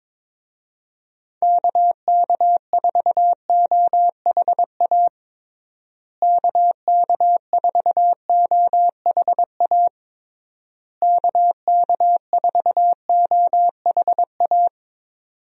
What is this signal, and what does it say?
Morse code, heard as a keyed tone, about 22 wpm, 700 Hz. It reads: KK4OHA KK4OHA KK4OHA
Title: USA Call Signs - 6 Characters - Repeated 3x - 22wpm